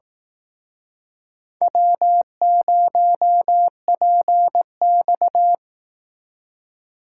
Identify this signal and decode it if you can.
Morse code, heard as a keyed tone, about 19 wpm, 700 Hz. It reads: W0PX
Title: Speed Racing - USA 1x2 Call Signs 15wpm